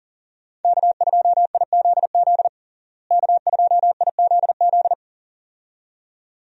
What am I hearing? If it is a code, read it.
Morse code, heard as a keyed tone, about 40 wpm, 700 Hz. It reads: K2IZZ K2IZZ
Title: USA Call Signs - 5 Characters - Repeated 2x - 40wpm